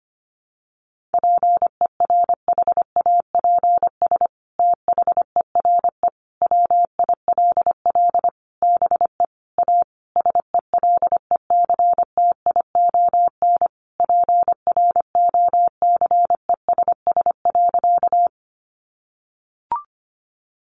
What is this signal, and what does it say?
Morse code, heard as a keyed tone, about 25 wpm, 700 Hz. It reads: PERHAPS THERE WILL BE A SELECTION PROCESS.
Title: Sentences from Top 1400 Words 25wpm